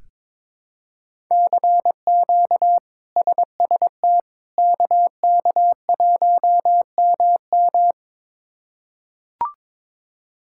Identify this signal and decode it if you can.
Morse code, heard as a keyed tone, about 22 wpm, 700 Hz. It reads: CQ SST KK1MM
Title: K1USN SST QSO Practice - 22wpm